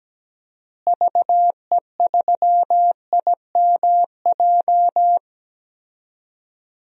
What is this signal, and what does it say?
Morse code, heard as a keyed tone, about 17 wpm, 700 Hz. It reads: VE3IMJ
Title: DXCC-100 Call Signs - 6 Characters - Repeated 2x - 17wpm